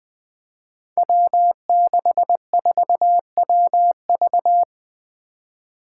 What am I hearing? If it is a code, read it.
Morse code, heard as a keyed tone, about 20 wpm, 700 Hz. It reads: W64WV
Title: Five Challenging Letter-Number Combinations 20wpm